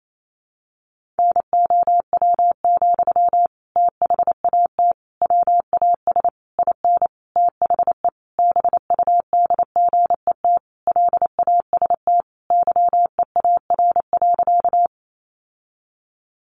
Morse code, heard as a keyed tone, about 28 wpm, 700 Hz. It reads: NOW, THAT WAS IN THE BUDGET LAST YEAR.
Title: Sentences from Top 1200 Words 28wpm